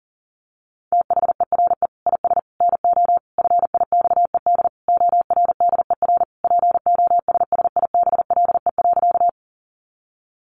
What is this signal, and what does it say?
Morse code, heard as a keyed tone, about 40 wpm, 700 Hz. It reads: THERE IS NO FIXED ORDER POSSIBLE.
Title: Sentences from Top 1100 Words 40wpm